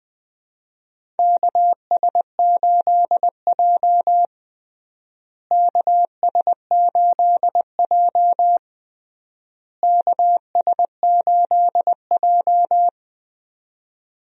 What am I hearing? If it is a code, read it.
Morse code, heard as a keyed tone, about 21 wpm, 700 Hz. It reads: KS8J KS8J KS8J
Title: USA Call Signs - 4 Characters - Repeated 3x - 20wpm